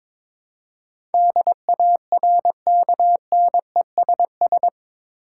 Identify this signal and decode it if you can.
Morse code, heard as a keyed tone, about 22 wpm, 700 Hz. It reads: DARKNESS
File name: practice / 22wpm / Eight Letter Words - 22wpm